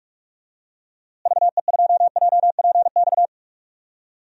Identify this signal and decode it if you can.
Morse code, heard as a keyed tone, about 46 wpm, 700 Hz. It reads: VE2JPX
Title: DXCC-100 Call Signs - 6 Characters - Repeated 2x - 45wpm